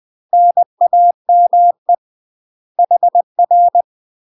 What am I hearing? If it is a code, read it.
Morse code, heard as a keyed tone, about 20 wpm, 700 Hz. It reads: NAME HR
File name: MASTER QSOs TRAINING/20WPM